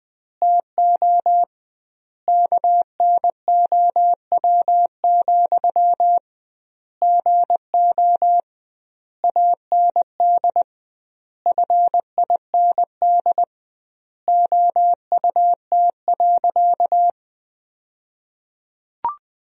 Morse code, heard as a keyed tone, about 20 wpm, 700 Hz. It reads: TO KNOW, GO AND FIND OUT.
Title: 2x Word-Spacing - Sentences from Top 200 Words - 20wpm